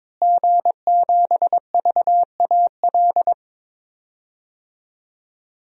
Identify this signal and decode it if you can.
Morse code, heard as a keyed tone, about 22 wpm, 700 Hz. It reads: G7VAL
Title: DXCC-100 Call Signs - 5 Characters - Repeated 2x - 22wpm